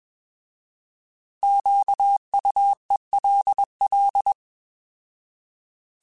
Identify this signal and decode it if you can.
Morse code, heard as a keyed tone, about 21 wpm, 800 Hz. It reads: QUELL
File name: Alphabetical / Alphabetical mp3s / Qs @20